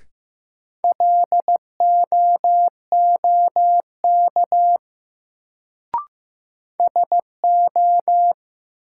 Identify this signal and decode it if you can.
Morse code, heard as a keyed tone, about 16 wpm, 700 Hz. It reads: LOOK SO
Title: Sentences from Top 100 Words - New Word Drill - 15wpm